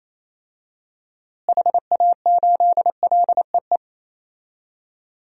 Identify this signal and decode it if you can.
Morse code, heard as a keyed tone, about 27 wpm, 700 Hz. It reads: HA8LEE
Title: DXCC-100 Call Signs - 6 Characters - Repeated 2x - 28wpm